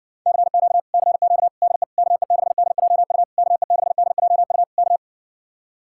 Morse code, heard as a keyed tone, about 61 wpm, 700 Hz. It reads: CQ CQ DE KE6DJU KE6DJU K